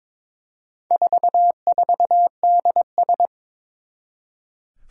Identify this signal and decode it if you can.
Morse code, heard as a keyed tone, about 22 wpm, 700 Hz. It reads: 44DS
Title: Four Challenging Letter-Number Combinations 22wpm